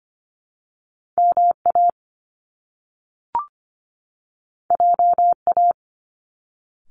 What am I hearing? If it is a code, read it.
Morse code, heard as a keyed tone, about 25 wpm, 700 Hz. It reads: M A J A
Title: Top Two Letter Combinations in Words 25wpm